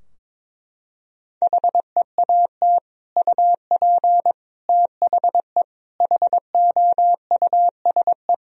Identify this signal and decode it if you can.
Morse code, heard as a keyed tone, about 22 wpm, 700 Hz. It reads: HEAT UP THE HOUSE
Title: Sentences from Top 1000 Words - Review Part 3 - 22wpm